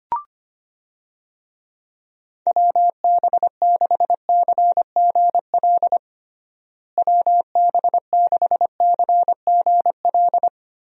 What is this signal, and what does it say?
Morse code, heard as a keyed tone, about 25 wpm, 700 Hz. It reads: WB6CGL WB6CGL
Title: USA Call Signs - 6 Characters - Repeated 2x - 25wpm